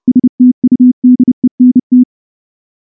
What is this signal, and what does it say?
Morse code, heard as a keyed tone, about 28 wpm, 250 Hz. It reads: STUDENT
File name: morse / morse-training-30WPM